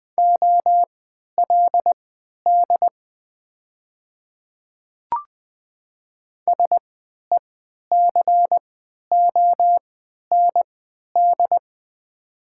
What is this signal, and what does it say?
Morse code, heard as a keyed tone, about 20 wpm, spaced as for 11 wpm, 700 Hz. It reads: OLD SECOND
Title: 3x Character Spacing - Top 300 Words - 20wpm